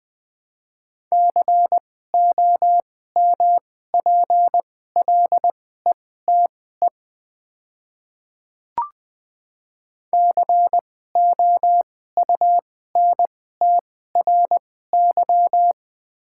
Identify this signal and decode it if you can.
Morse code, heard as a keyed tone, about 20 wpm, spaced as for 14 wpm, 700 Hz. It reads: COMPLETE COUNTRY
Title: 2x Character Spacing - Top 500 Words - 20wpm